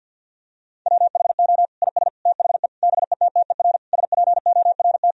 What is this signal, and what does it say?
Morse code, heard as a keyed tone, about 48 wpm, 700 Hz. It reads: WHO IS THE BETTER SPORT
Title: Sentences from Top 1000 Words - Review Part 4 - 50wpm